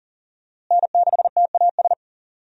Morse code, heard as a keyed tone, about 40 wpm, 700 Hz. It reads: N6TAS
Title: USA Call Signs - 5 Characters - Repeated 2x - 40wpm